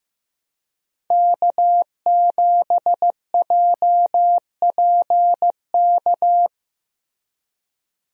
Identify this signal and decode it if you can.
Morse code, heard as a keyed tone, about 15 wpm, 700 Hz. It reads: K7JPK
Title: USA Call Signs - 5 Characters - Repeated 2x - 15wpm